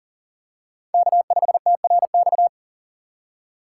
Morse code, heard as a keyed tone, about 40 wpm, 700 Hz. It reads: K5TRX
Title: USA Call Signs - 5 Characters - 3x Character Spacing - 40wpm